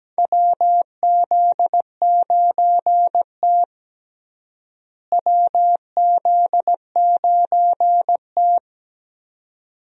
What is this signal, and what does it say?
Morse code, heard as a keyed tone, about 17 wpm, 700 Hz. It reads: WZ9T WZ9T
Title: USA Call Signs - 4 Characters - Repeated 2x - 17wpm